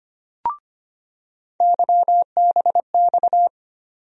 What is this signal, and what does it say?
Morse code, heard as a keyed tone, about 25 wpm, 700 Hz. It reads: YBX
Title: USA Call Sign Postfixes - 3 Letters 25wpm